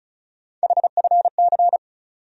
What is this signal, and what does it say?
Morse code, heard as a keyed tone, about 36 wpm, 700 Hz. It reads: HFC
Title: USA Call Sign Postfixes - 3 Letters 35wpm